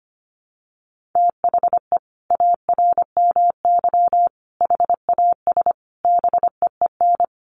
Morse code, heard as a keyed tone, about 25 wpm, 700 Hz. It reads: THE ARMY HAS BEEN
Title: Sentences from Top 1400 Words 25wpm